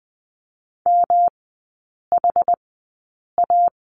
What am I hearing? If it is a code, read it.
Morse code, heard as a keyed tone, about 20 wpm, spaced as for 8 wpm, 700 Hz. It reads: MHA
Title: 127 Lesson 21e - Number 6 - Three Characters 20wpm